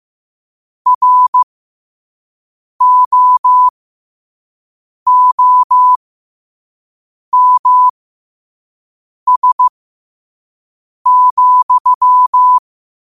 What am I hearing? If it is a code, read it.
Morse code, heard as a keyed tone, about 15 wpm, spaced as for 5 wpm, 1000 Hz. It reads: ROOMS,
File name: qotdfiles / 5wpm